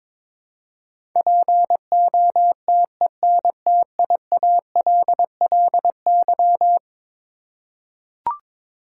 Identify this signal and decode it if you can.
Morse code, heard as a keyed tone, about 22 wpm, 700 Hz. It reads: POTENTIALLY
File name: practice / 22wpm / Eleven Letter Words - 22wpm